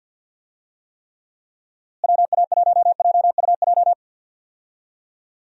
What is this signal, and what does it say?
Morse code, heard as a keyed tone, about 50 wpm, 700 Hz. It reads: WA1JUJ
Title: USA Call Signs - 6 Characters - Repeated 3x - 50wpm